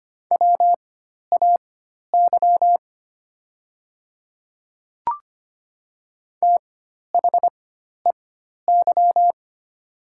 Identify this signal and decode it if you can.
Morse code, heard as a keyed tone, about 25 wpm, spaced as for 12 wpm, 700 Hz. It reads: WAY THEY
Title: 4x Character Spacing - Top 300 Words - 25wpm